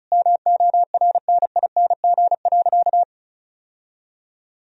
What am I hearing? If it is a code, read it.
Morse code, heard as a keyed tone, about 36 wpm, 700 Hz. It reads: MORNING.
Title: Sentences from Top 1000 Words - Review Part 2 - 35wpm